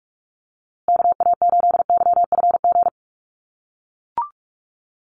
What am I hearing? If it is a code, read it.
Morse code, heard as a keyed tone, about 44 wpm, 700 Hz. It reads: KA8YFG
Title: USA Call Signs Repeated 2x 45wpm